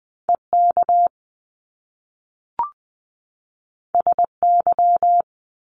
Morse code, heard as a keyed tone, about 20 wpm, 700 Hz. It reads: E K S Y
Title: Top Two Letter Combinations in Words 20wpm